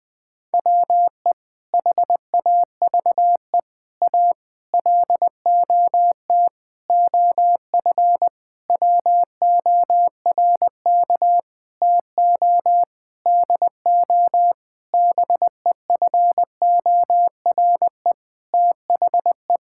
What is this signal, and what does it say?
Morse code, heard as a keyed tone, about 20 wpm, 700 Hz. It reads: WE HAVE A LOT OF WORK TO DO BEFORE THE
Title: Sentences with 8-Word Phrases - 20wpm